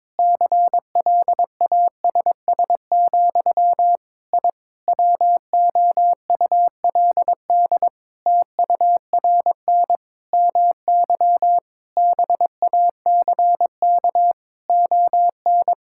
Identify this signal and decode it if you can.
Morse code, heard as a keyed tone, about 22 wpm, 700 Hz. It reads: CLASS, I WOULD TURN MY BACK ON
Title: Sentences from Top 1000 Words - Review Part 6 - 22wpm